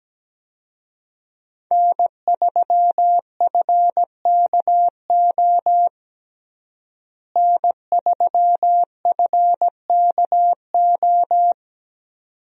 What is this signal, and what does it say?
Morse code, heard as a keyed tone, about 17 wpm, 700 Hz. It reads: N3FKO N3FKO
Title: USA Call Signs - 5 Characters - Repeated 2x - 17wpm